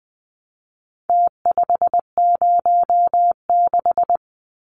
Morse code, heard as a keyed tone, about 20 wpm, 700 Hz. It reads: T506
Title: Numbers - Contest Style - 4-Digit 20wpm